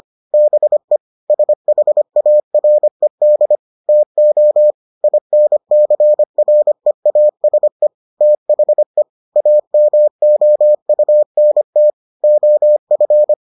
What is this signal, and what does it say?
Morse code, heard as a keyed tone, about 25 wpm, 600 Hz. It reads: BE SHARED TO INCREASE THE AMOUNT OF